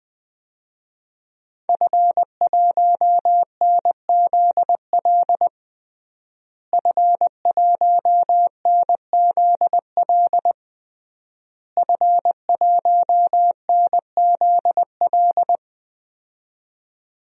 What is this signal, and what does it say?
Morse code, heard as a keyed tone, about 20 wpm, 700 Hz. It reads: F1NZL F1NZL F1NZL
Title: DXCC-100 Call Signs - 5 Characters - Repeated 3x - 20wpm